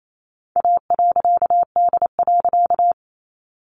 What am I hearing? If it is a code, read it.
Morse code, heard as a keyed tone, about 28 wpm, 700 Hz. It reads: A.D.
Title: A Sound Of Thunder 28wpm